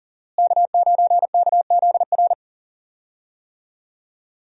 Morse code, heard as a keyed tone, about 40 wpm, 700 Hz. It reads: K9KZR
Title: USA Call Signs - 5 Characters - Repeated 2x - 40wpm